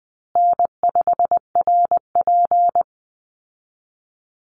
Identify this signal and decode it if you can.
Morse code, heard as a keyed tone, about 20 wpm, 700 Hz. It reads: N5RP
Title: Call Signs Repeated 3x - 20wpm